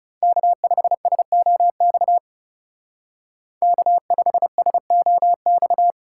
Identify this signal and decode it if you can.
Morse code, heard as a keyed tone, about 37 wpm, 700 Hz. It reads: K5SOX K5SOX
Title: Speed Racing - USA 1x3 Call Signs 30wpm